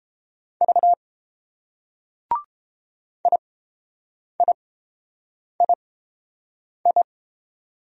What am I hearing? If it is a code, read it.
Morse code, heard as a keyed tone, about 30 wpm, spaced as for 7 wpm, 700 Hz. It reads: V IIII